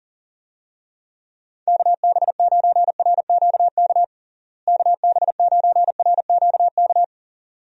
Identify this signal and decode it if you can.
Morse code, heard as a keyed tone, about 40 wpm, 700 Hz. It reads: KB9RQK KB9RQK
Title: USA Call Signs - 6 Characters - Repeated 2x - 40wpm